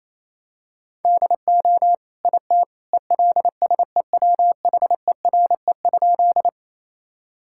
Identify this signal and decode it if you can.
Morse code, heard as a keyed tone, about 28 wpm, 700 Hz. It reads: DO IT ELSEWHERE?
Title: Sentences from Top 1600 Words 28wpm